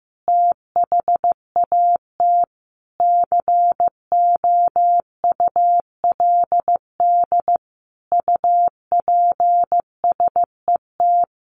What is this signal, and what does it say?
Morse code, heard as a keyed tone, about 15 wpm, 700 Hz. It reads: THAT COULD UPSET
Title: A Sound Of Thunder 15wpm